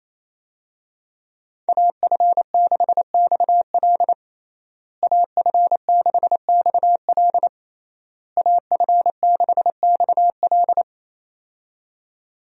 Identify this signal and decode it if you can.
Morse code, heard as a keyed tone, about 27 wpm, 700 Hz. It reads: AF6XL AF6XL AF6XL
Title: USA Call Signs - 5 Characters - Repeated 3x - 28wpm